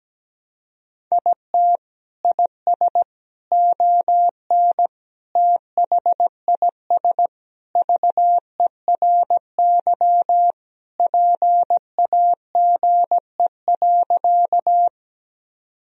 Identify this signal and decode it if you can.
Morse code, heard as a keyed tone, about 17 wpm, 700 Hz. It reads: IT IS ON THIS VERY PAGE.
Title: Sentences from Top 1000 Words - Review Part 1 - 17wpm